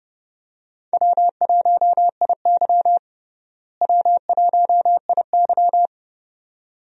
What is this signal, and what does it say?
Morse code, heard as a keyed tone, about 30 wpm, 700 Hz. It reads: W1IY W1IY
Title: USA Call Signs - 4 Characters - Repeated 2x - 30wpm